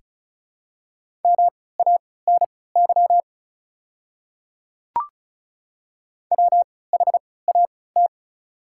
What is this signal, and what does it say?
Morse code, heard as a keyed tone, about 36 wpm, spaced as for 20 wpm, 690 Hz. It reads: MANY WHAT
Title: 3x Character Spacing - Top 100 Words - 35wpm